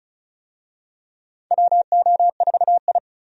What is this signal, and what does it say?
Morse code, heard as a keyed tone, about 36 wpm, 700 Hz. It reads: WO4I
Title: USA Call Signs - 4 Characters - 35wpm